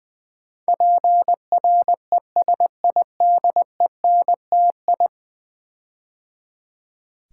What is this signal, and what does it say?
Morse code, heard as a keyed tone, about 20 wpm, 700 Hz. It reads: PRESIDENTI
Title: Build A Word - 5 Letters At A Time - 12-Letter Words - 20wpm